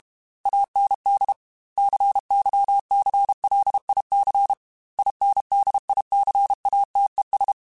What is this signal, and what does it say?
Morse code, heard as a keyed tone, about 32 wpm, 800 Hz. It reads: AND CYCLIC INDICATES
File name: Technical Subjects / Technical Subjects-mp3s / XL @30